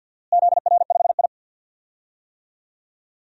Morse code, heard as a keyed tone, about 51 wpm, 700 Hz. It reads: ZR5I
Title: DXCC-100 Call Signs - 4 Characters - Repeated 2x - 50wpm